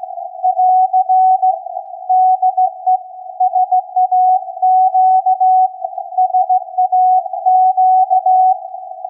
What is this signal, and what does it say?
Morse code, heard as a keyed tone, about 16 wpm, 740 Hz. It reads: <AR> DE SAQ SAQ
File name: iz7slz / VLF / SAQ 16-11-2022